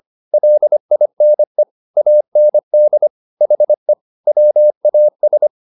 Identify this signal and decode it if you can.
Morse code, heard as a keyed tone, about 25 wpm, 600 Hz. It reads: LINE AND HE WAS
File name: the old man and the sea part 1 25wpm